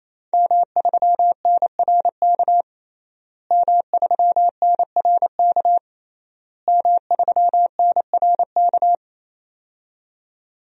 Morse code, heard as a keyed tone, about 28 wpm, 690 Hz. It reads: M3NRK M3NRK M3NRK
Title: DXCC-100 Call Signs - 5 Characters - Repeated 3x - 28wpm